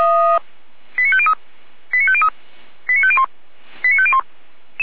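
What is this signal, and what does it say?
Five-tone selective call (ZVEI), heard as quick runs of five tones, tones of about 70 ms. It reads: 89503 89503 89501 89501
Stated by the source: ZVEI five-tone call, code 89305, received by SDR, from Feuerwehr Orlamünde.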